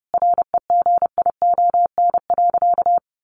Morse code, heard as a keyed tone, about 30 wpm, 700 Hz. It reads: REGION.
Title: Sentences from Top 1500 Words 30wpm